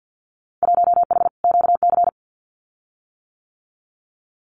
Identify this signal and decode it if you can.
Morse code, heard as a keyed tone, about 50 wpm, 700 Hz. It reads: 1H QC
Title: Winter Field Day Exchanges 50wpm